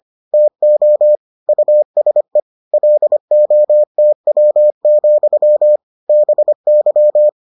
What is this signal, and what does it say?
Morse code, heard as a keyed tone, about 25 wpm, 600 Hz. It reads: TO USE LOTW, BY